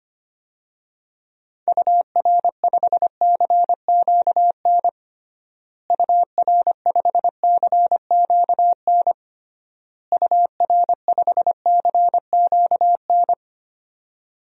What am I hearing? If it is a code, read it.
Morse code, heard as a keyed tone, about 25 wpm, 700 Hz. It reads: UR5CQN UR5CQN UR5CQN
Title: DXCC-100 Call Signs - 6 Characters - Repeated 3x - 25wpm